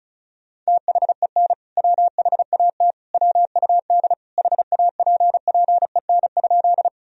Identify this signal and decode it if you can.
Morse code, heard as a keyed tone, about 36 wpm, 700 Hz. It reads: THEN WHAT WUD HAPPEN?
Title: Sentences from Top 400 Words and Abbreviations 35wpm